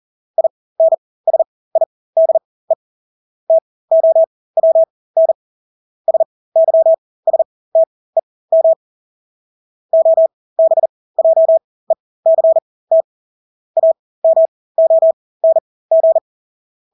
Morse code, heard as a keyed tone, about 40 wpm, spaced as for 20 wpm, 650 Hz. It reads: INSIDE TOWN SYSTEM OBJECT AMONG
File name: Engelske ord 40-20 wpm